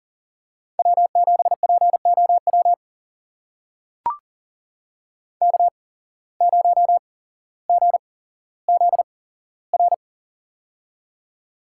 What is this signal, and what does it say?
Morse code, heard as a keyed tone, about 40 wpm, spaced as for 17 wpm, 700 Hz. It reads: W7POW K0GZR
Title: USA Call Signs - 5 Characters - 8x Character Spacing - 40wpm